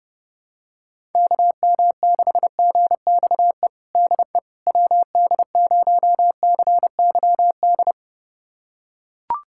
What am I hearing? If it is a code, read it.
Morse code, heard as a keyed tone, about 30 wpm, 700 Hz. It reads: KM6GXE DE WD0CYD
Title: USA Callsign DE Callsign - 6 Characters - 30wpm